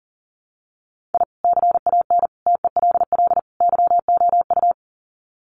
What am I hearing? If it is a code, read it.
Morse code, heard as a keyed tone, about 40 wpm, 700 Hz. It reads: I CAN TELL YOU
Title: Sets of 4 Words 40wpm